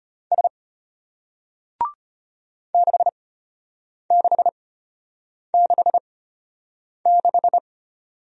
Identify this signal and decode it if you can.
Morse code, heard as a keyed tone, about 34 wpm, spaced as for 8 wpm, 700 Hz. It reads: S 6666